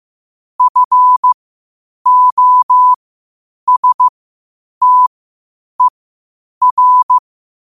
Morse code, heard as a keyed tone, about 15 wpm, spaced as for 8 wpm, 1000 Hz. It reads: FOSTER
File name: qotdfiles / 8wpm